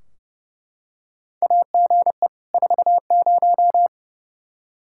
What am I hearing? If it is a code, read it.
Morse code, heard as a keyed tone, about 30 wpm, 700 Hz. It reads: AGE 40
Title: Practice QSO - Exchange 3 - More Info - 30wpm